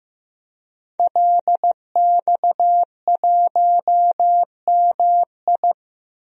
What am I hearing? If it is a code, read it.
Morse code, heard as a keyed tone, about 16 wpm, 700 Hz. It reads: LX1MI
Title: DXCC-100 Call Signs - 5 Characters - Repeated 2x - 15wpm